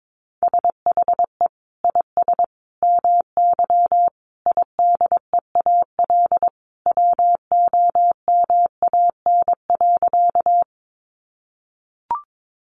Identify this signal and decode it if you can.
Morse code, heard as a keyed tone, about 22 wpm, 700 Hz. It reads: SHE IS MY IDEAL WOMAN.
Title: Sentences from Top 1400 Words 22wpm